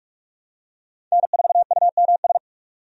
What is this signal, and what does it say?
Morse code, heard as a keyed tone, about 46 wpm, 700 Hz. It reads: N4UMS
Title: USA Call Signs - 5 Characters - 45wpm